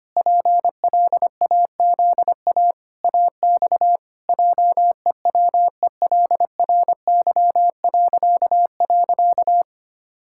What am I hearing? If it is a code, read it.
Morse code, heard as a keyed tone, about 25 wpm, 700 Hz. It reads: PLAZA AX JEWELRY..
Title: Pangram Challenge v2 - 25wpm